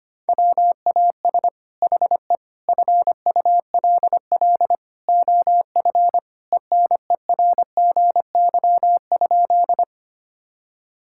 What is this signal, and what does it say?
Morse code, heard as a keyed tone, about 25 wpm, 700 Hz. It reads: WAS HE FULL OF ENERGY?
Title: Sentences from Top 1000 Words - Review Part 6 - 25wpm